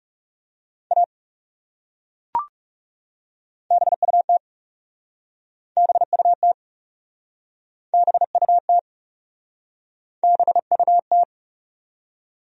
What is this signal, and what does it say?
Morse code, heard as a keyed tone, about 41 wpm, 700 Hz. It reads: A BUT BUT BUT BUT